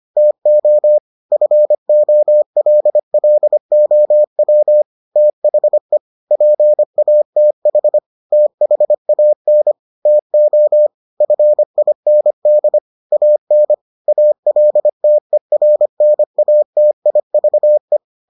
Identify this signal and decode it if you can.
Morse code, heard as a keyed tone, about 25 wpm, 600 Hz. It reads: TO FOLLOW THE PATH THAN TO FIND AN ALTERNATIVE